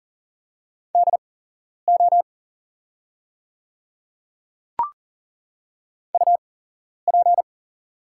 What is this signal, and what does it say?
Morse code, heard as a keyed tone, about 40 wpm, spaced as for 11 wpm, 700 Hz. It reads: DO UP
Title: 8x Character Spacing - Top 100 Words - 40wpm